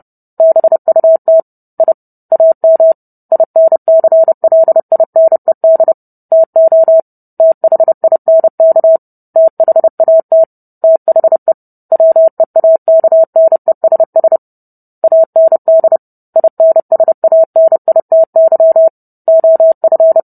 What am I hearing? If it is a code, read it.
Morse code, heard as a keyed tone, about 30 wpm, 650 Hz. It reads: BUT I AM INCLINED TO THINK THAT THE WEAKNESS AND INSANITY OF